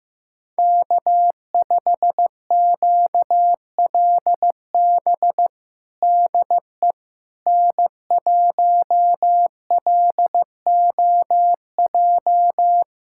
Morse code, heard as a keyed tone, about 15 wpm, 700 Hz. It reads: K5QLB DE N1LOJ
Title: USA Callsign DE Callsign - 5 Characters - 15wpm